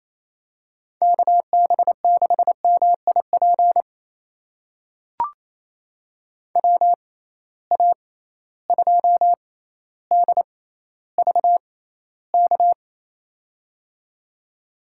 Morse code, heard as a keyed tone, about 27 wpm, spaced as for 14 wpm, 700 Hz. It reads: KB6MIP WA2DVK
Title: USA Call Signs - 6 Characters - 6x Character Spacing - 28wpm